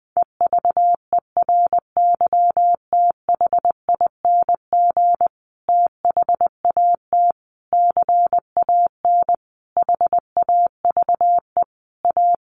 Morse code, heard as a keyed tone, about 20 wpm, 700 Hz. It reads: EVERYTHING THAT CAN HAVE A
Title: Sentences from Top 1300 Words 20wpm